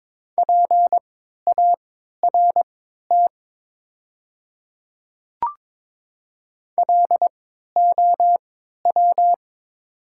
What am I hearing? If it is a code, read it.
Morse code, heard as a keyed tone, about 22 wpm, spaced as for 13 wpm, 700 Hz. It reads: PART LOW
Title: 3x Character Spacing - Top 300 Words - 22wpm